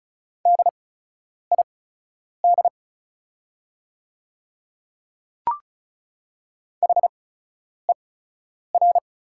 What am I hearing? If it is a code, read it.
Morse code, heard as a keyed tone, about 34 wpm, spaced as for 10 wpm, 690 Hz. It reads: DID HER
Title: 8x Character Spacing - Top 100 Words - 35wpm